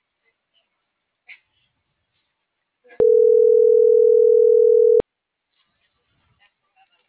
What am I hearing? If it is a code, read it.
Ringback tone (North America), 440 + 480 Hz.